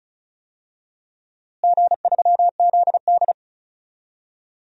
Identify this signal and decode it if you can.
Morse code, heard as a keyed tone, about 36 wpm, 700 Hz. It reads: G3ZD
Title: DXCC-100 Call Signs - 4 Characters - Repeated 3x - 35wpm